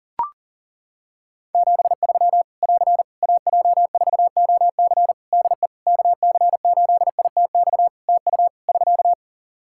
Morse code, heard as a keyed tone, about 42 wpm, 700 Hz. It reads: 73 <AR> AJ4OC DE KC8ITX TU <SK>
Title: Practice QSO - Exchange 4 - Ending - 40wpm